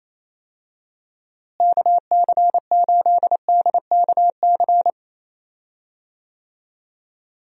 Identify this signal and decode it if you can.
Morse code, heard as a keyed tone, about 27 wpm, 700 Hz. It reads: KC8DKC
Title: USA Call Signs - 6 Characters - Repeated 3x - 28wpm